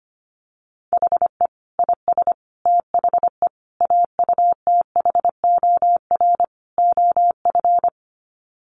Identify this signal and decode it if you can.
Morse code, heard as a keyed tone, about 25 wpm, 700 Hz. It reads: HE IS THE AUTHOR OF
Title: Sets of 5 Words Encore2 25wpm